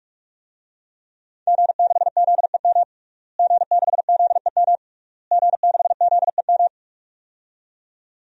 Morse code, heard as a keyed tone, about 46 wpm, 700 Hz. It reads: G6ZEM G6ZEM G6ZEM
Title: DXCC-100 Call Signs - 5 Characters - Repeated 3x - 45wpm